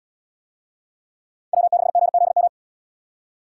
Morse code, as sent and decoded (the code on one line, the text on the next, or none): .-- -.... --. --.. --
W6GZM